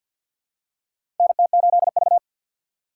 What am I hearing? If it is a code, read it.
Morse code, heard as a keyed tone, about 50 wpm, 690 Hz. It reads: NT8V
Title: USA Call Signs - 4 Characters - 6x Character Spacing - 50wpm